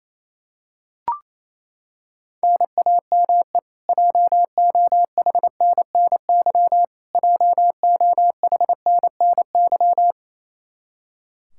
Morse code, heard as a keyed tone, about 28 wpm, 700 Hz. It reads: NAME JOHNNY JOHNNY
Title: Practice QSO - Exchange 1 - Beginning - 28wpm